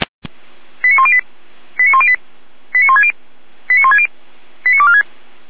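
Five-tone selective call (ZVEI), heard as quick runs of five tones, tones of about 70 ms. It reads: 89108 89108 89160 89160 89246
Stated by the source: ZVEI five-tone selective call received by SDR.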